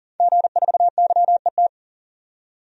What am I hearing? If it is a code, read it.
Morse code, heard as a keyed tone, about 40 wpm, 700 Hz. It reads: G4YET
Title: DXCC-100 Call Signs - 5 Characters - 40wpm